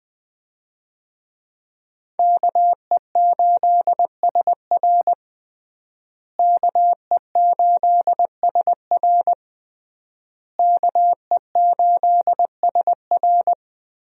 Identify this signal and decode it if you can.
Morse code, heard as a keyed tone, about 20 wpm, 700 Hz. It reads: KE8SR KE8SR KE8SR
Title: USA Call Signs - 5 Characters - Repeated 3x - 20wpm